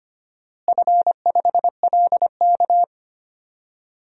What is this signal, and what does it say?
Morse code, heard as a keyed tone, about 25 wpm, 700 Hz. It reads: F5LK
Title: DXCC-100 Call Signs - 4 Characters - Repeated 3x - 25wpm